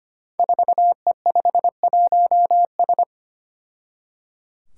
Morse code, heard as a keyed tone, about 25 wpm, 700 Hz. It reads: 4E51S
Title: Five Challenging Letter-Number Combinations 25wpm